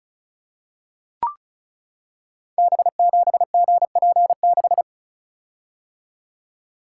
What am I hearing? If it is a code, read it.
Morse code, heard as a keyed tone, about 36 wpm, 700 Hz. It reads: B7GP6
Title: Code Groups - 5 Letter:Numbers - 35wpm